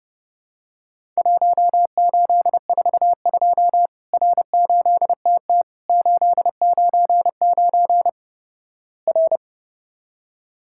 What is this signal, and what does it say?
Morse code, heard as a keyed tone, about 32 wpm, 700 Hz. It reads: 1842 R8TT 899 R
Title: Worked All Europe DX Contest - QTC Exchange - 30wpm